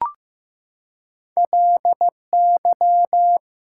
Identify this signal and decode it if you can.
Morse code, heard as a keyed tone, about 15 wpm, 700 Hz. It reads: LY
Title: Prefixes and Suffixes 15wpm